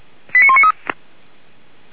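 Five-tone selective call (ZVEI), heard as quick runs of five tones, tones of about 70 ms. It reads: 89193
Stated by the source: Received on SDR, ZVEI code 89193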